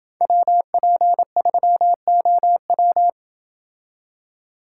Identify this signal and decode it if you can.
Morse code, heard as a keyed tone, about 27 wpm, 700 Hz. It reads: WP3OW